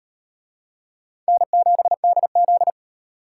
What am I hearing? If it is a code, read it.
Morse code, heard as a keyed tone, about 38 wpm, 700 Hz. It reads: N7DZ